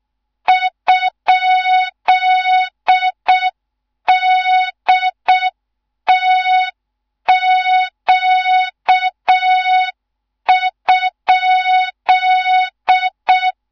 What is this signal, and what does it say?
Morse code, heard as a keyed tone, about 6 wpm, 3700 Hz. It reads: ?DTQ?
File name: Morze-HA2MN / hang / 14 280 HA2MN